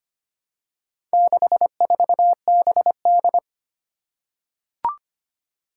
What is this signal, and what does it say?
Morse code, heard as a keyed tone, about 25 wpm, 700 Hz. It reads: 64BD
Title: Four Challenging Letter-Number Combinations 25wpm